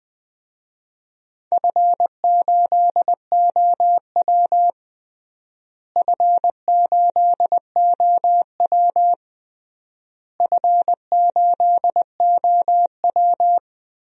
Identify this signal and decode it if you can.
Morse code, heard as a keyed tone, about 20 wpm, 700 Hz. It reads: F8OW F8OW F8OW
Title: DXCC-100 Call Signs - 4 Characters - Repeated 3x - 20wpm